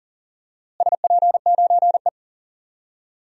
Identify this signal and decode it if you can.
Morse code, heard as a keyed tone, about 38 wpm, 700 Hz. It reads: SP9E
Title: DXCC-100 Call Signs - 4 Characters - Repeated 3x - 40wpm